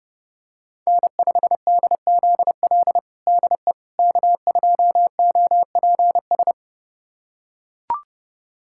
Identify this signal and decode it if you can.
Morse code, heard as a keyed tone, about 32 wpm, 700 Hz. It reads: N5DZL DE K2OPS
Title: USA Callsign DE Callsign - 5 Characters - 30wpm